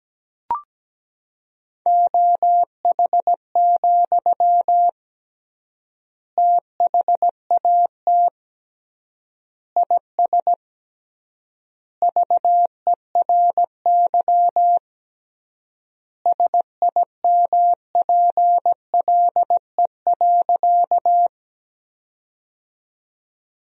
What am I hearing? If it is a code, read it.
Morse code, heard as a keyed tone, about 17 wpm, 700 Hz. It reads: OH, THAT IS VERY SIMPLE.
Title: 3x Word-Spacing - Sentences from Top 500 Words - 17wpm